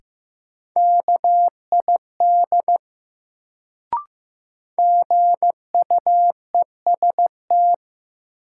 Morse code, heard as a keyed tone, about 15 wpm, 700 Hz. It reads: KID GUEST
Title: Sentences from Top 1600 Words - New Word Drill Alt - 15wpm